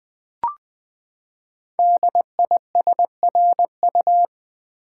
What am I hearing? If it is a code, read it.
Morse code, heard as a keyed tone, about 20 wpm, 700 Hz. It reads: DISRU
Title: Build A Word - 5 Letters At A Time - 10-Letter Words - 20wpm